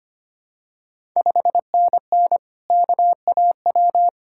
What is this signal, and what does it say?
Morse code, heard as a keyed tone, about 25 wpm, 700 Hz. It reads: 5NN KAW
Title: Ontario QSO Party Exchange 25wpm